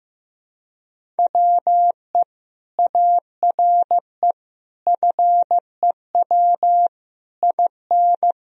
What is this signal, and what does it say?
Morse code, heard as a keyed tone, about 16 wpm, 700 Hz. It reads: WE ARE FEW IN
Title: Sentences from Top 1000 Words - Review Part 1 - 15wpm